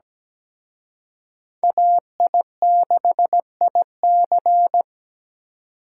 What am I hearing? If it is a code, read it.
Morse code, heard as a keyed tone, about 20 wpm, 700 Hz. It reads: AI6IC
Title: Speed Racing - USA 2x2 Call Signs 17wpm